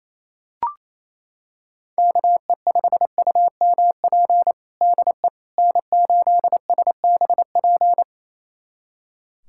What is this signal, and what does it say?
Morse code, heard as a keyed tone, about 28 wpm, 700 Hz. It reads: KE5UMP DE N8SBP
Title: USA Callsign DE Callsign 28wpm